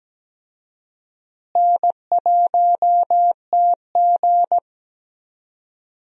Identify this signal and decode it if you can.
Morse code, heard as a keyed tone, about 17 wpm, 700 Hz. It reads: N1TG
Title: USA Call Signs - 4 Characters - Repeated 3x - 17wpm